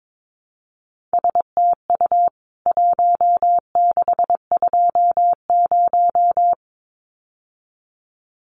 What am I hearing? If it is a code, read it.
Morse code, heard as a keyed tone, about 22 wpm, 700 Hz. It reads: STU 1620
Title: CWT Exchange 22wpm